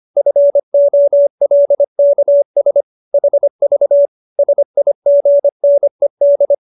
Morse code, heard as a keyed tone, about 25 wpm, 570 Hz. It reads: FOLKS HV SIGNED